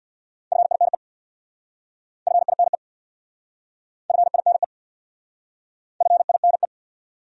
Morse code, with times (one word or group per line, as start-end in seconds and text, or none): 0.5-0.9 FINE
2.3-2.8 FINE
4.1-4.6 FINE
6.0-6.7 FINE